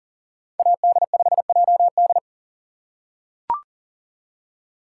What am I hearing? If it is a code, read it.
Morse code, heard as a keyed tone, about 40 wpm, 700 Hz. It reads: AD5JD
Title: USA Call Signs - 5 Characters - 2x Character Spacing - 40wpm